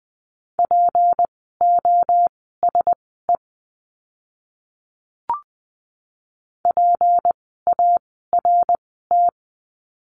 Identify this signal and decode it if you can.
Morse code, heard as a keyed tone, about 20 wpm, spaced as for 15 wpm, 700 Hz. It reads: POSE PART
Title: 153 Lesson 25f - Letter p - Words 20wpm